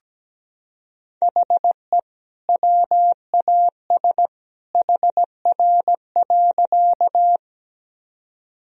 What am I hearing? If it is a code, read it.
Morse code, heard as a keyed tone, about 17 wpm, 700 Hz. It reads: HE WAS HR.
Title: Sentences from Top 200 Words and Abbreviations 17wpm